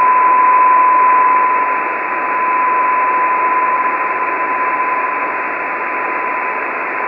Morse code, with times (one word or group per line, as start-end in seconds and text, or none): none